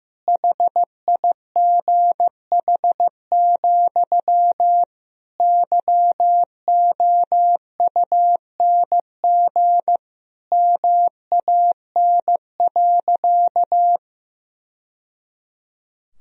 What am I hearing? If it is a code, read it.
Morse code, heard as a keyed tone, about 15 wpm, 700 Hz. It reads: HIGH, YOUNG MAN.
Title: Sentences from Top 1000 Words - Review Part 2 - 15wpm